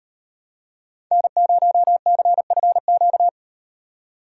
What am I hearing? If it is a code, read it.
Morse code, heard as a keyed tone, about 38 wpm, 700 Hz. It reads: N0CFQ